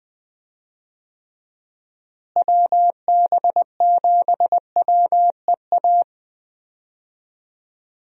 Morse code, heard as a keyed tone, about 20 wpm, 700 Hz. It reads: WB7WEA
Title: USA Call Signs - 6 Characters - Repeated 2x - 20wpm